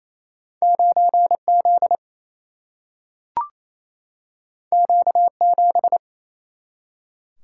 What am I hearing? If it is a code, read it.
Morse code, heard as a keyed tone, about 28 wpm, 700 Hz. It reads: 9Z Q7
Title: Single-Letter Single-Number 28wpm